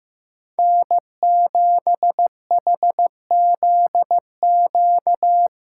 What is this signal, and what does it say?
Morse code, heard as a keyed tone, about 15 wpm, 700 Hz. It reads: N7HZQ
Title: USA Call Signs - 5 Characters - 8x Character Spacing - 15wpm